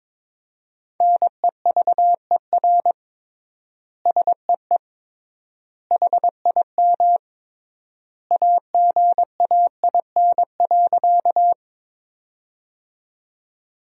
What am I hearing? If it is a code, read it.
Morse code, heard as a keyed tone, about 22 wpm, 700 Hz. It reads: NEVER SEE HIM AGAIN.
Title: 3x Word-Spacing - Sentences from Top 300 Words - 22wpm